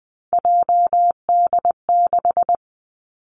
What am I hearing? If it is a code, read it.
Morse code, heard as a keyed tone, about 20 wpm, 700 Hz. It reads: JD6
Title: Three Challenging Letter-Number Combinations 20wpm